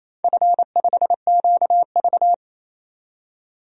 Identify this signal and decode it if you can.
Morse code, heard as a keyed tone, about 28 wpm, 700 Hz. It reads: F5QV
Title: DXCC-100 Call Signs - 4 Characters - 28wpm